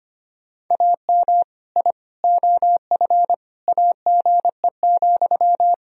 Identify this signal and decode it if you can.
Morse code, heard as a keyed tone, about 25 wpm, 700 Hz. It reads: AM I OF AGE,
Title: Sentences from Top 1000 Words - Review Part 3 - 25wpm